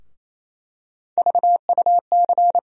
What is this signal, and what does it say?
Morse code, heard as a keyed tone, about 28 wpm, 700 Hz. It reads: VUC
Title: USA Call Sign Postfixes - 3 Letters 28wpm